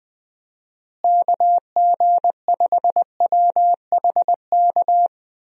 Morse code, heard as a keyed tone, about 21 wpm, 700 Hz. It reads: KG5WHK
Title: USA Call Signs - 6 Characters - Repeated 2x - 20wpm